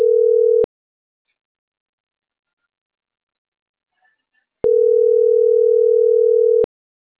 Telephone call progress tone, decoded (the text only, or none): ringback tone